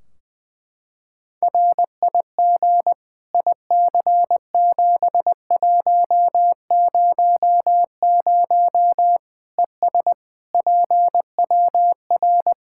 Morse code, heard as a keyed tone, about 20 wpm, 700 Hz. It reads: RIG IC7100 ES PWR
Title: Practice QSO - Exchange 2 - Info - 20wpm